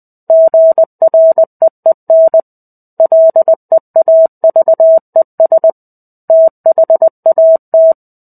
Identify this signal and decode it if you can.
Morse code, heard as a keyed tone, about 20 wpm, 650 Hz. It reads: GREEN LEAVES THAT
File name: 05-Chapter-05 20 WPM